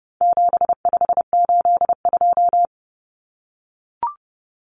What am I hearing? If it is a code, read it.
Morse code, heard as a keyed tone, about 30 wpm, 700 Hz. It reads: 7582
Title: Numbers - 4-Digit 30wpm